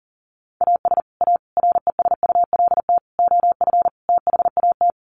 Morse code, heard as a keyed tone, about 40 wpm, 700 Hz. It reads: AS A RESULT OF THAT
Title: Sets of 5 Words Encore 40wpm